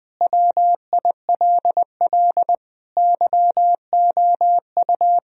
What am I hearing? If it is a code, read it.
Morse code, heard as a keyed tone, about 20 wpm, 700 Hz. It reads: WILL YOU
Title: Sentences from Top 1000 Words - Review Part 2 - 20wpm